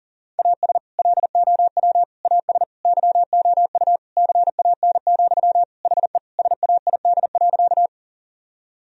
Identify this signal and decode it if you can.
Morse code, heard as a keyed tone, about 40 wpm, 700 Hz. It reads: AS LOW AS YOU CAN, HE SAID.
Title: Sentences from Top 1000 Words - Review Part 1 - 40wpm